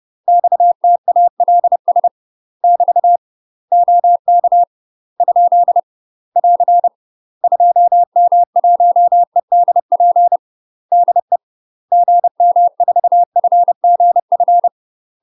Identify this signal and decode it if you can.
Morse code, heard as a keyed tone, about 30 wpm, 700 Hz. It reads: XTALS <BT> OK ? <AR> 2M1EDP DE GM4FGF
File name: UK MASTER QSOs TRAINING / 30WPM / UK MASTER QSOs 0049